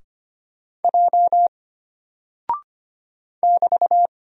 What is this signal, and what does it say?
Morse code, heard as a keyed tone, about 25 wpm, 700 Hz. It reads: J <BT>
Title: Single Character (Letter:Number:Punctuation:Prosign) Alt 25wpm